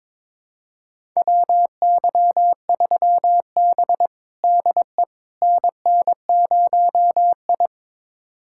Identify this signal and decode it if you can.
Morse code, heard as a keyed tone, about 21 wpm, 700 Hz. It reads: WY3B DE NN0I
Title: USA Callsign DE Callsign - 4 Characters - 22wpm